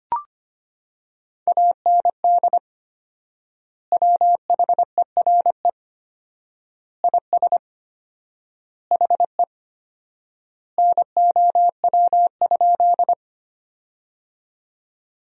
Morse code, heard as a keyed tone, about 25 wpm, 700 Hz. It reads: AND WHERE IS HE NOW?
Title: 4x Word-Spacing - Sentences from Top 200 Words - 25wpm